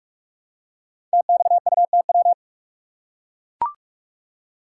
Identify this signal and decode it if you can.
Morse code, heard as a keyed tone, about 44 wpm, 690 Hz. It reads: TXUTW
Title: Code Groups - 5 Letters - 45wpm